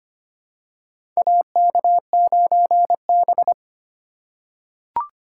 Morse code, heard as a keyed tone, about 25 wpm, 700 Hz. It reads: AK9B
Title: USA Call Signs - 4 Characters - 2x Character Spacing - 25wpm